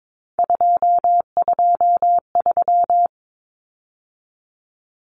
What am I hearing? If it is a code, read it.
Morse code, heard as a keyed tone, about 22 wpm, 700 Hz. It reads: 223
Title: Common Cut Numbers - 3-Digit 22wpm